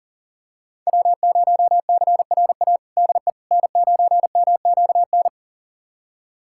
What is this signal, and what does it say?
Morse code, heard as a keyed tone, about 40 wpm, 700 Hz. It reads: W0CRA DE N9MQN
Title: USA Callsign DE Callsign - 5 Characters - 40wpm